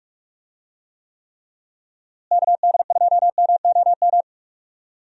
Morse code, heard as a keyed tone, about 46 wpm, 700 Hz. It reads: KD2MOM
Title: USA Call Signs - 6 Characters - Repeated 3x - 45wpm